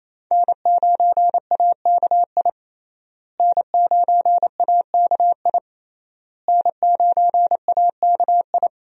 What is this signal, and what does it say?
Morse code, heard as a keyed tone, about 28 wpm, 700 Hz. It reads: N9AKI N9AKI N9AKI
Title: USA Call Signs - 5 Characters - Repeated 3x - 28wpm